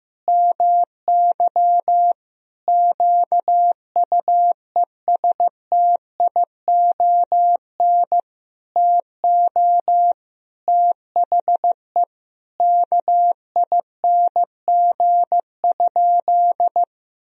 Morse code, heard as a keyed tone, about 16 wpm, 700 Hz. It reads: MY QUESTION TO THE KING?
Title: Sentences from Top 1000 Words - Review Part 2 - 15wpm